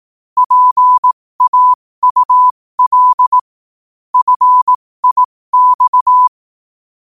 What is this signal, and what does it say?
Morse code, heard as a keyed tone, about 18 wpm, spaced as for 15 wpm, 1000 Hz. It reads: PAUL FIX
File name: qotdfiles / 15wpm